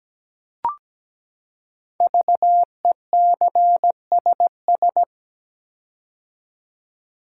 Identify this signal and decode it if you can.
Morse code, heard as a keyed tone, about 17 wpm, 700 Hz. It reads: VECSS
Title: Code Groups - 5 Letters - 17wpm